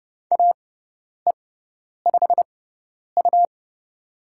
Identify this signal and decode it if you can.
Morse code, heard as a keyed tone, about 30 wpm, spaced as for 10 wpm, 700 Hz. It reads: AE5U
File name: Lesson 17g - Letter u - Call Signs 30wpm